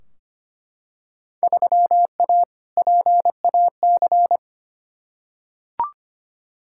Morse code, heard as a keyed tone, about 25 wpm, 700 Hz. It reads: 3A PAC
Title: Field Day Exchanges - 25wpm